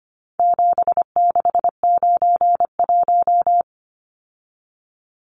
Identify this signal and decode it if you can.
Morse code, heard as a keyed tone, about 25 wpm, 700 Hz. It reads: 7691
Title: Numbers - 4-Digit 25wpm